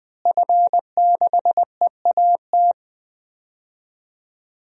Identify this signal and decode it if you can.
Morse code, heard as a keyed tone, about 20 wpm, 700 Hz. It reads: F6EAT
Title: DXCC-100 Call Signs - 5 Characters - Repeated 3x - 20wpm